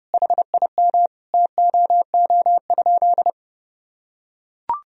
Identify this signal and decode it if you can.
Morse code, heard as a keyed tone, about 30 wpm, 700 Hz. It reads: HIM TOO?
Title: Sentences from Top 500 Words and Abbreviations 30wpm